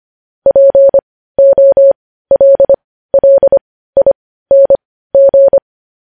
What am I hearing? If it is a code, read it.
Morse code, heard as a keyed tone, about 25 wpm, spaced as for 15 wpm, 550 Hz. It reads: POLLING
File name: podcast / News headlines-1x-25x15